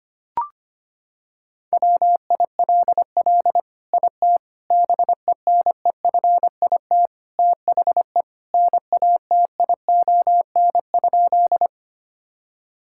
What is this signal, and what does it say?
Morse code, heard as a keyed tone, about 25 wpm, 700 Hz. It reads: WILL IT BENEFIT THE NATION?
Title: Sentences from Top 1000 Words - Review Part 6 - 25wpm